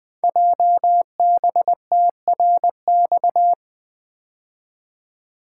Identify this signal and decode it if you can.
Morse code, heard as a keyed tone, about 20 wpm, 700 Hz. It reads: JBTRX
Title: Code Groups - 5 Letters - 20wpm